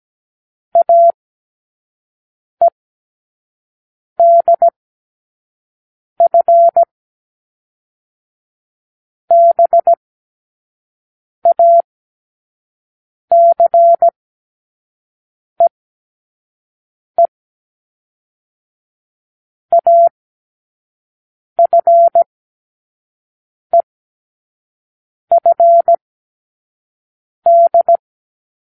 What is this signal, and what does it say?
Morse code, heard as a keyed tone, about 17 wpm, 680 Hz. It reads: A E D F B A C E E A F E F D
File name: k7qo / Lesson9(A-F)